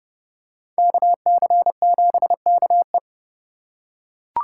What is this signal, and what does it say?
Morse code, heard as a keyed tone, about 29 wpm, 700 Hz. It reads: KC7KE